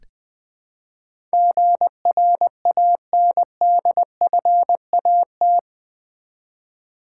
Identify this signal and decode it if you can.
Morse code, heard as a keyed tone, about 20 wpm, 700 Hz. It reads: GRANDFAT
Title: Build A Word - 4 Letters At A Time - 11-Letter Words - 20wpm